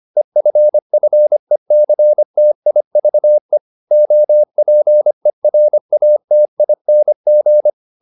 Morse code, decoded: EFFECTIVE OPERATING

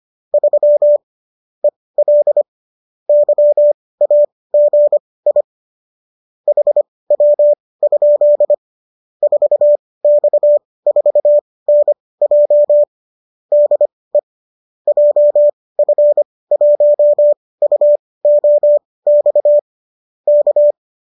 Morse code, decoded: 3 EL YAGI HW? 4X4NJ DE JF1UOX K

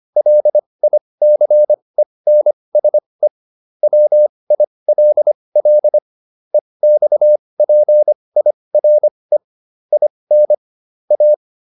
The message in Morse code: LICENSE WILL EXPIRE IN A

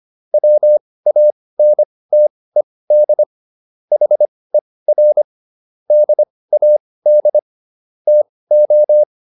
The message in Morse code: WANTED HER DAD TO

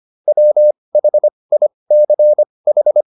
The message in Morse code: WHICH

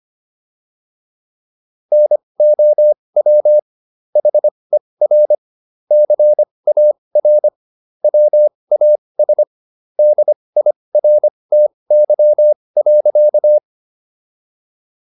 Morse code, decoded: NOW HER CAR WAS DIRTY.

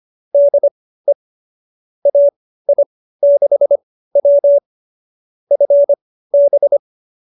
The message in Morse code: DE AI6W FB

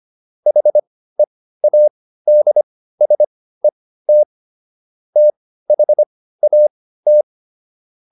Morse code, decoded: HEADSET THAT

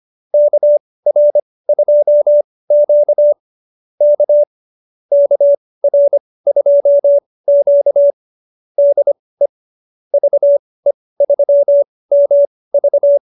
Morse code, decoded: KR2Q K KR2Q DE VE3MV

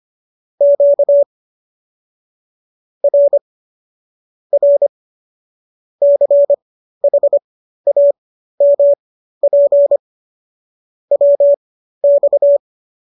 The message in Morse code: QRR CHAMP WX

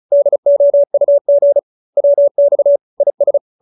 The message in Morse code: DOUG WX IS

